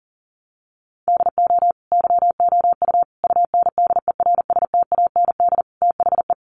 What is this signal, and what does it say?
Morse code, heard as a keyed tone, about 40 wpm, 700 Hz. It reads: DO YOU UNDERSTAND THE
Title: Sentences from Top 1200 Words 40wpm